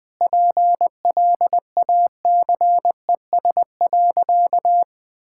Morse code, heard as a keyed tone, about 21 wpm, 700 Hz. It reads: PLACES.
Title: Sentences with 4-Word Phrases - 20wpm